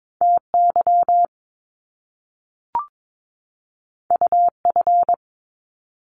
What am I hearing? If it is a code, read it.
Morse code, heard as a keyed tone, about 22 wpm, 700 Hz. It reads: TY UF
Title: Top Two Letter Combinations in Words 22wpm